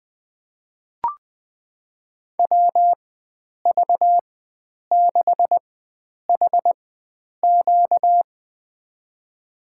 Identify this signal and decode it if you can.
Morse code, heard as a keyed tone, about 21 wpm, spaced as for 9 wpm, 700 Hz. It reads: WV6HQ
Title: USA Call Signs - 5 Characters - 4x Character Spacing - 20wpm